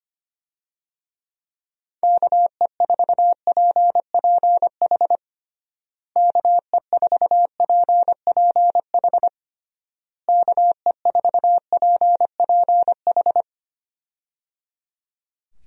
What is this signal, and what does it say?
Morse code, heard as a keyed tone, about 25 wpm, 700 Hz. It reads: KE4PPH KE4PPH KE4PPH
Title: USA Call Signs - 6 Characters - Repeated 3x - 25wpm